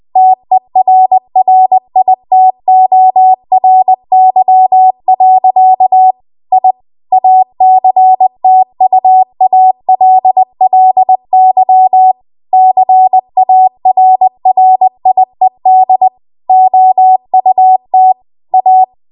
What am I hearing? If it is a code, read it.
Morse code, heard as a keyed tone, about 20 wpm, 750 Hz. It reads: TERRITORY. I ACTUALLY CARRIED OUT A
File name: Archive / 20 WPM / 20WPM